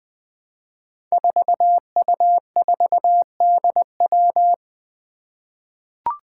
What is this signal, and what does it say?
Morse code, heard as a keyed tone, about 21 wpm, 700 Hz. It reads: 4U4DW
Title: Five Challenging Letter-Number Combinations 20wpm